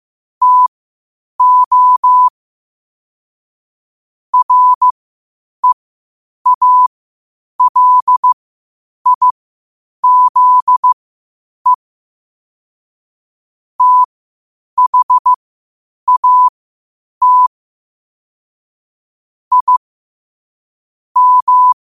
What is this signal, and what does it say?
Morse code, heard as a keyed tone, about 15 wpm, spaced as for 8 wpm, 1000 Hz. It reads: TO REALIZE THAT I M
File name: qotdfiles / 8wpm